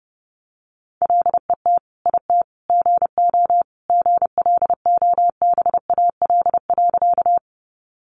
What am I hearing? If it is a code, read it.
Morse code, heard as a keyed tone, about 30 wpm, 700 Hz. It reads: LET IT GO GLOBAL.
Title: Sentences from Top 1200 Words 30wpm